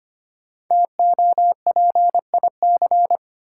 Morse code, heard as a keyed tone, about 25 wpm, 700 Hz. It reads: TOPIC
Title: Sentences from Top 1200 Words - New Word Drill Alt - 25wpm